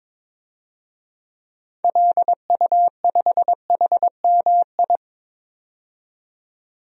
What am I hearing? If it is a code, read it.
Morse code, heard as a keyed tone, about 22 wpm, 700 Hz. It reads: LU5HMI
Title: DXCC-100 Call Signs - 6 Characters - 22wpm